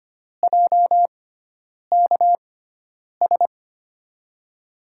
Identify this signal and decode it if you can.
Morse code, heard as a keyed tone, about 25 wpm, spaced as for 9 wpm, 700 Hz. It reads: JKS
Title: USA Call Signs - 5 Characters - 6x Character Spacing - 25wpm